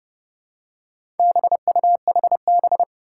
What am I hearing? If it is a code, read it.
Morse code, heard as a keyed tone, about 30 wpm, 700 Hz. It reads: BUHB
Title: Four Challenging Letter-Number Combinations 30wpm